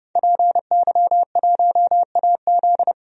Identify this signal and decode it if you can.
Morse code, heard as a keyed tone, about 30 wpm, 700 Hz. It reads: PY1AZ
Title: DXCC-100 Call Signs - 5 Characters - 30wpm